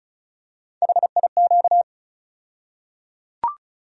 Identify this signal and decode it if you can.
Morse code, heard as a keyed tone, about 34 wpm, 700 Hz. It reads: HIQ
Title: USA Call Sign Postfixes - 3 Letters 35wpm